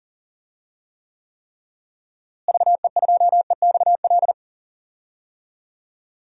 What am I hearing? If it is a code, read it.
Morse code, heard as a keyed tone, about 39 wpm, 700 Hz. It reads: VE2EXL
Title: DXCC-100 Call Signs - 6 Characters - Repeated 3x - 40wpm